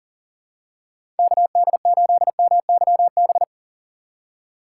Morse code, heard as a keyed tone, about 40 wpm, 700 Hz. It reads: KD8MYB
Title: USA Call Signs - 6 Characters - 3x Character Spacing - 40wpm